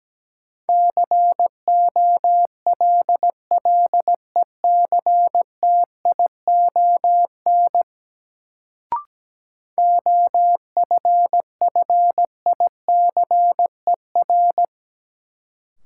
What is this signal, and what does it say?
Morse code, heard as a keyed tone, about 17 wpm, 700 Hz. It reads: COLLECTION OFFICER
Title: Sentences from Top 1200 Words - New Word Drill - 17wpm